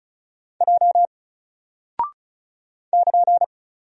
Morse code, heard as a keyed tone, about 36 wpm, 700 Hz. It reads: J <KN>
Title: Single Character (Letter:Number:Punctuation:Prosign) Alt 35wpm